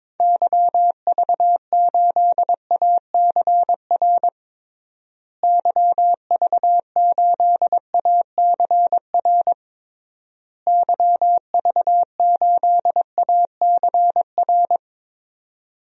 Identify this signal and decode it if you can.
Morse code, heard as a keyed tone, about 22 wpm, 700 Hz. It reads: YV8ACR YV8ACR YV8ACR
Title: DXCC-100 Call Signs - 6 Characters - Repeated 3x - 22wpm